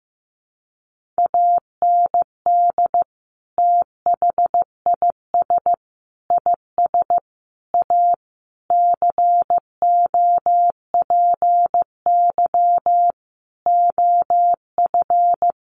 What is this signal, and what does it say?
Morse code, heard as a keyed tone, about 15 wpm, 700 Hz. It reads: AND THIS IS A COPY OF
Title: Sentences from Top 1400 Words 15wpm